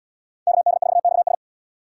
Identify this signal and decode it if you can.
Morse code, heard as a keyed tone, about 68 wpm, 700 Hz. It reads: KD4XS